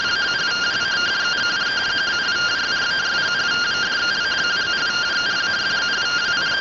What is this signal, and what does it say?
Two-tone teleprinter signal, about 170 Hz shift.